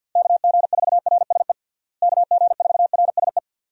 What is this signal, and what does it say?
Morse code, heard as a keyed tone, about 50 wpm, 700 Hz. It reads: KG4RSE KG4RSE
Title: USA Call Signs - 6 Characters - Repeated 2x - 50wpm